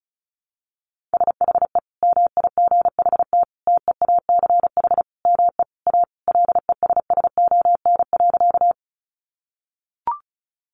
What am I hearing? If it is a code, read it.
Morse code, heard as a keyed tone, about 34 wpm, 700 Hz. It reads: SHE MIGHT TEACH ME A LESSON.
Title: Sentences from Top 1200 Words 35wpm